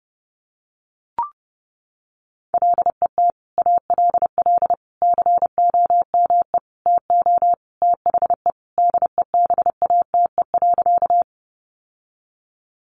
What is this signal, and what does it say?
Morse code, heard as a keyed tone, about 30 wpm, 700 Hz. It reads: LET ALL COME TO THE DEBATE.
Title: Sentences from Top 1100 Words 30wpm